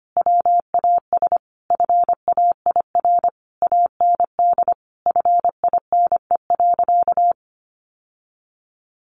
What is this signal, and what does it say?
Morse code, heard as a keyed tone, about 25 wpm, 700 Hz. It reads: WAS FAIR AND FINE.
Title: Sentences from Top 1300 Words 25wpm